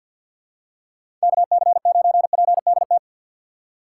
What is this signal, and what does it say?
Morse code, heard as a keyed tone, about 53 wpm, 700 Hz. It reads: KC9PDT
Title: USA Call Signs - 6 Characters - 3x Character Spacing - 50wpm